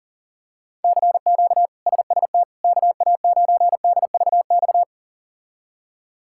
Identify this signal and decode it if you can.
Morse code, heard as a keyed tone, about 40 wpm, 700 Hz. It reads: CQ SST KA9DVX
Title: K1USN SST QSO Practice - 40wpm